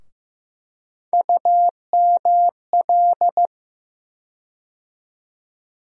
Morse code, heard as a keyed tone, about 15 wpm, 700 Hz. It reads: UML
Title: USA Call Sign Postfixes - 3 Letters 15wpm